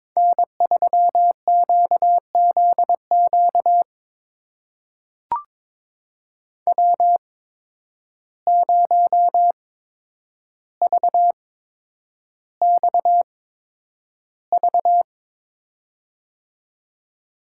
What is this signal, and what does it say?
Morse code, heard as a keyed tone, about 22 wpm, spaced as for 9 wpm, 700 Hz. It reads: N3QZQ W0VXV
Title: USA Call Signs - 5 Characters - 8x Character Spacing - 22wpm